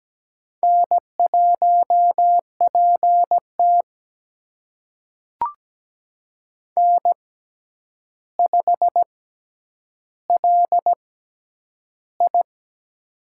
Morse code, heard as a keyed tone, about 17 wpm, spaced as for 9 wpm, 700 Hz. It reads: N1PT N5LI
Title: USA Call Signs - 6x Character Spacing - 17wpm